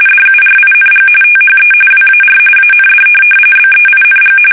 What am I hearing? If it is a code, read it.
RTTY, about 45.45 baud. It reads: DURING THE WAR AND BEING
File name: fsk850 / benjamin-g-lamme